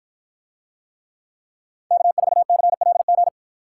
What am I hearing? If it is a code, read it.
Morse code, heard as a keyed tone, about 53 wpm, 700 Hz. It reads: K4CLG